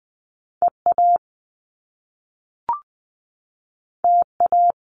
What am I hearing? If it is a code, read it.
Morse code, heard as a keyed tone, about 20 wpm, 700 Hz. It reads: EA TA
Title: Top Two Letter Combinations in Words 20wpm